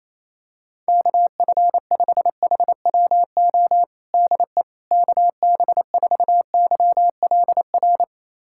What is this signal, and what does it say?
Morse code, heard as a keyed tone, about 28 wpm, 700 Hz. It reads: KF5HWO DE KB4YLR
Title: USA Callsign DE Callsign - 6 Characters - 28wpm